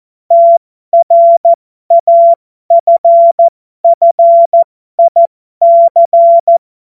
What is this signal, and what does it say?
Morse code, heard as a keyed tone, about 14 wpm, spaced as for 12 wpm, 680 Hz. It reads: TRAFFIC
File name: Morsekurs / Witze